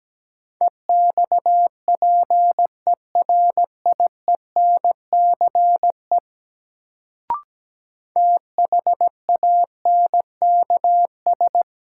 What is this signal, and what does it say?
Morse code, heard as a keyed tone, about 17 wpm, 700 Hz. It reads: EXPERIENCE THANKS
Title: Sentences from Top 600 Words - New Word Drill - 17wpm